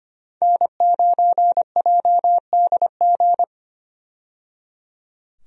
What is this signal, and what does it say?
Morse code, heard as a keyed tone, about 25 wpm, 700 Hz. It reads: N9JDG
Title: USA Call Signs - 5 Characters - 25wpm